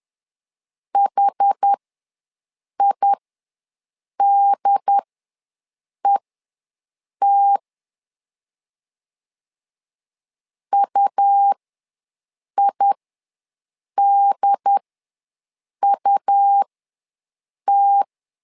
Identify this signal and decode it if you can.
Morse code, heard as a keyed tone, about 10 wpm, spaced as for 6 wpm, 800 Hz. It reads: HIDET UIDUT
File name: LEZIONI / 2 UHDN / 2 UHD ws 10wpm 2058